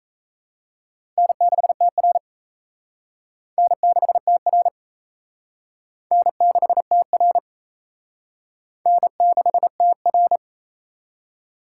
Morse code, heard as a keyed tone, about 35 wpm, 700 Hz. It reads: N6TR N6TR N6TR N6TR
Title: Speed Racing - USA 1x2 Call Signs 28wpm